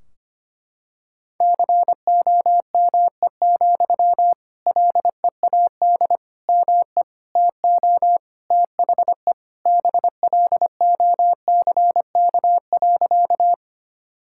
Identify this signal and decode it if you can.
Morse code, heard as a keyed tone, about 25 wpm, 700 Hz. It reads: COME, LEAD ME TO THE BLOCK.
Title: Sentences from Top 1600 Words 25wpm